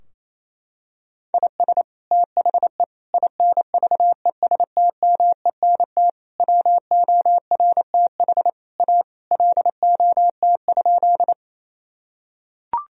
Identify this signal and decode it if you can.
Morse code, heard as a keyed tone, about 28 wpm, 700 Hz. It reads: IS THE INVESTMENT WORTH A LOT?
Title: Sentences from Top 1000 Words - Review Part 4 - 28wpm